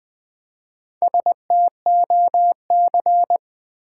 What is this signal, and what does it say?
Morse code, heard as a keyed tone, about 20 wpm, 700 Hz. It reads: STOC
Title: Build A Word - 1 Letter At A Time - 5-Letter Words - 20wpm